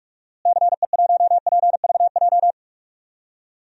Morse code, heard as a keyed tone, about 46 wpm, 700 Hz. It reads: CE1PVJ
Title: DXCC-100 Call Signs - 6 Characters - Repeated 3x - 45wpm